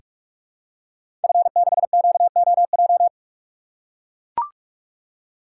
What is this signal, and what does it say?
Morse code, heard as a keyed tone, about 44 wpm, 700 Hz. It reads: F6QOJ
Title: Code Groups - 5 Letter:Numbers - 45wpm